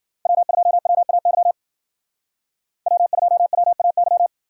Morse code, heard as a keyed tone, about 52 wpm, 700 Hz. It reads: W2WAY W2WAY